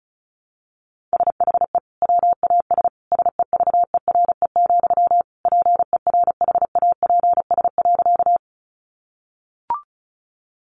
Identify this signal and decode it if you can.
Morse code, heard as a keyed tone, about 34 wpm, 700 Hz. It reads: SHE WAS SEVERE, PERHAPS.
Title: Sentences from Top 1400 Words 35wpm